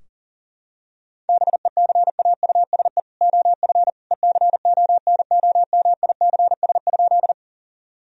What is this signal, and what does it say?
Morse code, heard as a keyed tone, about 40 wpm, 700 Hz. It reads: BECAUSE OF ECONOMICS?
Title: Sentences from Top 1000 Words - Review Part 3 - 40wpm